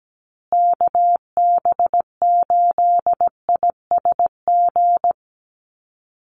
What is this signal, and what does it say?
Morse code, heard as a keyed tone, about 17 wpm, 700 Hz. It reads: KB8ISG
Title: USA Call Signs Repeated 2x 17wpm